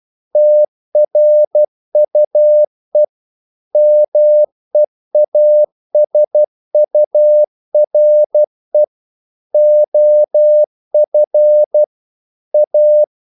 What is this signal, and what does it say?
Morse code, heard as a keyed tone, about 12 wpm, 600 Hz. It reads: TRUE MEASURE OF A
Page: English quotations in CW (telegraphy - morse)